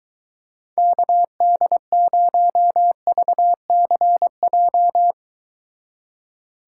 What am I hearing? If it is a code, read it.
Morse code, heard as a keyed tone, about 23 wpm, 700 Hz. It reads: KD0VCJ